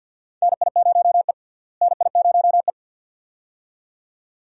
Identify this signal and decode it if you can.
Morse code, heard as a keyed tone, about 50 wpm, 700 Hz. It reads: NI0E NI0E
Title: USA Call Signs - 4 Characters - Repeated 2x - 50wpm